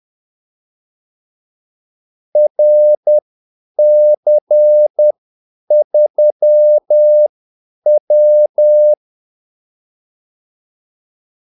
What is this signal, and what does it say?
Morse code, heard as a keyed tone, about 10 wpm, spaced as for 8 wpm, 600 Hz. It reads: RC3W
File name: Callsigns-10-8wpm